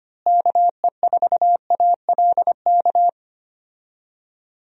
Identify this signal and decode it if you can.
Morse code, heard as a keyed tone, about 25 wpm, 700 Hz. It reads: KE4ALK
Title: USA Call Signs - 6 Characters - Repeated 3x - 25wpm